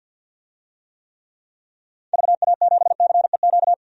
Morse code, heard as a keyed tone, about 52 wpm, 700 Hz. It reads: VA7CEQ
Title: DXCC-100 Call Signs - 6 Characters - 50wpm